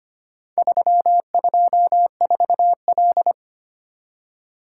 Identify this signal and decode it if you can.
Morse code, heard as a keyed tone, about 25 wpm, 700 Hz. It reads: 324L
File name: practice / 25wpm / Four Letter-Number Combinations - 25wpm